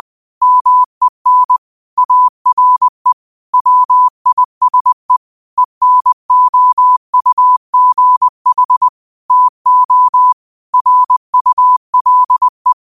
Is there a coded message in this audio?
Yes: Morse code from 0.4 to 12.7 s, heard as a keyed tone, about 20 wpm, 1000 Hz. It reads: MEN ARE WISE ENOUGH TO RULE